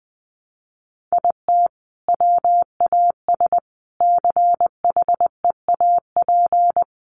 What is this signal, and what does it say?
Morse code, heard as a keyed tone, about 19 wpm, 700 Hz. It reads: IT WAS CHEAP
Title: Sentences from Top 1200 Words 20wpm